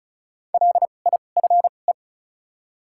Morse code, heard as a keyed tone, about 36 wpm, spaced as for 25 wpm, 700 Hz. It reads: LIFE
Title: 2x Character Spacing - Top 500 Words - 35wpm